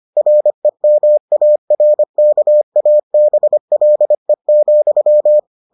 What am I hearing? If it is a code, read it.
Morse code, heard as a keyed tone, about 25 wpm, 600 Hz. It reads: REMARKABLE,